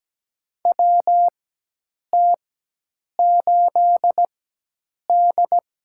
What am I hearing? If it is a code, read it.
Morse code, heard as a keyed tone, about 17 wpm, spaced as for 8 wpm, 700 Hz. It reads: WT8D
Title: USA Call Signs - 4x Character Spacing - 17wpm